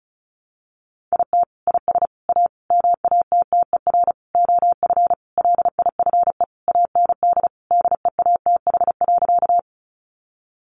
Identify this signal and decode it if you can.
Morse code, heard as a keyed tone, about 34 wpm, 700 Hz. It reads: IT IS A MATTER OF LIFE AND DEATH.
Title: Sentences from Top 1100 Words 35wpm